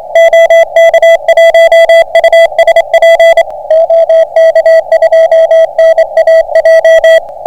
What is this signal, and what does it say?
Morse code, heard as a keyed tone, about 25 wpm, 660 Hz. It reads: OK1USP OK2NAJ